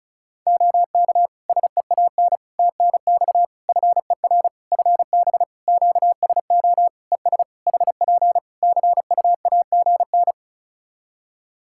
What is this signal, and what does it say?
Morse code, heard as a keyed tone, about 36 wpm, 700 Hz. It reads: OK SEAN TNX FER FB QSO ES HP CUAGN
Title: Practice QSO - Exchange 4 - Ending - 35wpm